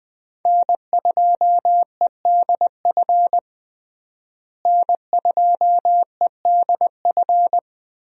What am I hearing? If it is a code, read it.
Morse code, heard as a keyed tone, about 20 wpm, 700 Hz. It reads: N2EDF N2EDF
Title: USA Call Signs - 5 Characters - Repeated 3x - 20wpm